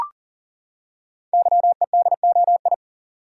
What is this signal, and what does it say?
Morse code, heard as a keyed tone, about 40 wpm, 690 Hz. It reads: YEDOI
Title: Code Groups - 5 Letters - 40wpm